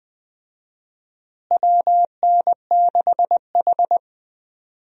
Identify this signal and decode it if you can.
Morse code, heard as a keyed tone, about 20 wpm, 700 Hz. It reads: WN6H
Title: USA Call Signs - 4 Characters - Repeated 2x - 20wpm